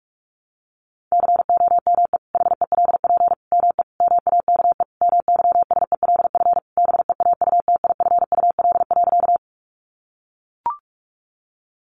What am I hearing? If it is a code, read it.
Morse code, heard as a keyed tone, about 44 wpm, 700 Hz. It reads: COME HELP ME MAKE MYSELF BEAUTIFUL.
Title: Sentences from Top 1300 Words 45wpm